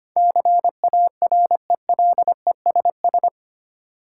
Morse code, heard as a keyed tone, about 25 wpm, 700 Hz. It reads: CARELESS
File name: 8 Letter Words - First 3 Letters Given - 25wpm